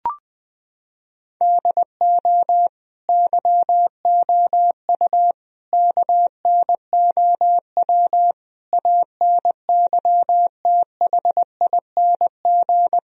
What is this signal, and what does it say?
Morse code, heard as a keyed tone, about 20 wpm, 700 Hz. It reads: DO YOU KNOW ANYTHING
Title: Sentences from Top 1000 Words - Review Part 1 - 20wpm